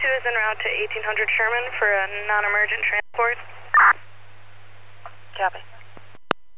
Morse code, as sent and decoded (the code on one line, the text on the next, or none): none